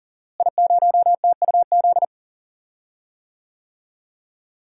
Morse code, heard as a keyed tone, about 40 wpm, 700 Hz. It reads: I0TUZ
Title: DXCC-100 Call Signs - 5 Characters - Repeated 3x - 40wpm